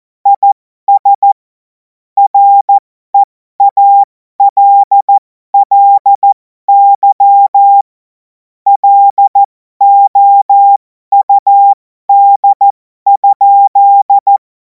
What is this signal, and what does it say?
Morse code, heard as a keyed tone, about 14 wpm, spaced as for 12 wpm, 800 Hz. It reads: IS REALLY LOUD?
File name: Morsekurs / Witze